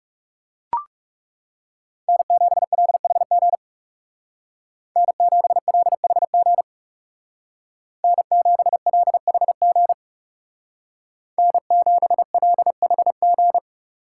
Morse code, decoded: N7LHG N7LHG N7LHG N7LHG

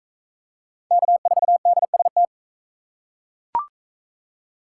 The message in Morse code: K4DST

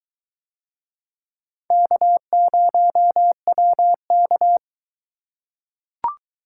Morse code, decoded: K0WK

